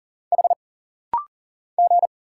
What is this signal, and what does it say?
Morse code, heard as a keyed tone, about 40 wpm, 700 Hz. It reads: H G
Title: Single Letters - Rapid-Fire 40wpm